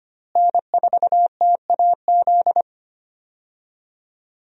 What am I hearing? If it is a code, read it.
Morse code, heard as a keyed tone, about 25 wpm, 690 Hz. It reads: N4TAZ
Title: USA Call Signs - 5 Characters - Repeated 2x - 25wpm